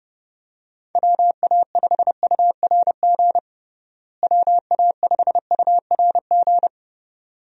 Morse code, heard as a keyed tone, about 32 wpm, 700 Hz. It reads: WA5URG WA5URG
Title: USA Call Signs - 6 Characters - Repeated 2x - 30wpm